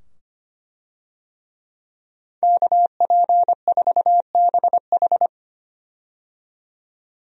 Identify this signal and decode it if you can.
Morse code, heard as a keyed tone, about 25 wpm, 700 Hz. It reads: KP4BH
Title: DXCC-100 Call Signs - 5 Characters - 25wpm